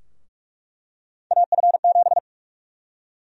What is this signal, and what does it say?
Morse code, heard as a keyed tone, about 46 wpm, 700 Hz. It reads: AF7
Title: USA Call Sign Prefixes - 45wpm